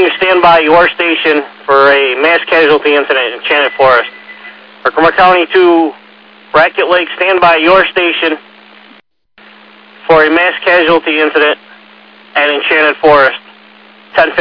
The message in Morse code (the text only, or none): none